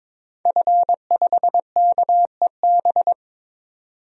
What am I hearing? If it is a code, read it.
Morse code, heard as a keyed tone, about 22 wpm, 700 Hz. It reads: F5KEB
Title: DXCC-100 Call Signs - 5 Characters - 22wpm